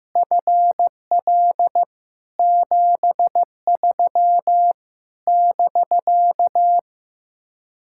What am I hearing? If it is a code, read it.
Morse code, heard as a keyed tone, about 15 wpm, 700 Hz. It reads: FL 73 <BK>
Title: Speed-Racing Common QSO Phrases Encore - 15wpm